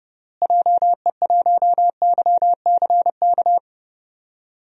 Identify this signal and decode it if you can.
Morse code, heard as a keyed tone, about 30 wpm, 700 Hz. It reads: JE1YCK
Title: DXCC-100 Call Signs - 6 Characters - Repeated 3x - 30wpm